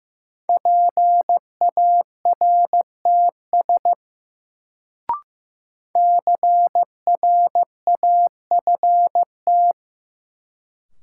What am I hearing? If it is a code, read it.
Morse code, heard as a keyed tone, about 15 wpm, 700 Hz. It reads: PARTS CRAFT
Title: Sentences from Top 1000 Words - Review Part 6 - New Word Drill - 15wpm